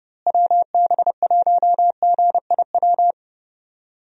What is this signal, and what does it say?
Morse code, heard as a keyed tone, about 31 wpm, 700 Hz. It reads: WB1GIW
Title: USA Call Signs - 6 Characters - Repeated 3x - 30wpm